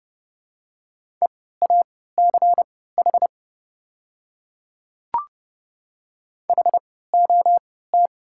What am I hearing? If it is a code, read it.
Morse code, heard as a keyed tone, about 30 wpm, spaced as for 17 wpm, 700 Hz. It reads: EACH HOT
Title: 3x Character Spacing - Top 100 Words - 30wpm